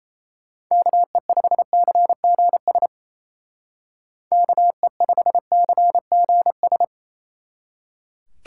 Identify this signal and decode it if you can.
Morse code, heard as a keyed tone, about 32 wpm, 700 Hz. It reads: KE5CGS KE5CGS
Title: Speed Racing - USA 2x3 Call Signs 28wpm